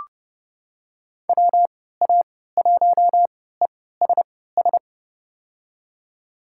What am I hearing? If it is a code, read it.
Morse code, heard as a keyed tone, about 29 wpm, spaced as for 17 wpm, 700 Hz. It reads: WA1ESS
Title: USA Call Signs - 3x Character Spacing - 30wpm